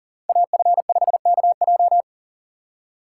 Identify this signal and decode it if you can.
Morse code, heard as a keyed tone, about 40 wpm, 700 Hz. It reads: AF5KJ
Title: USA Call Signs - 5 Characters - 3x Character Spacing - 40wpm